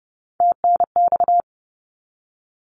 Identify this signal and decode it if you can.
Morse code, heard as a keyed tone, about 30 wpm, 700 Hz. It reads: TNX
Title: Abbreviations 30wpm